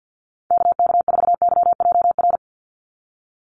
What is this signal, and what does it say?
Morse code, heard as a keyed tone, about 49 wpm, 700 Hz. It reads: KK4YJR
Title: USA Call Signs 50wpm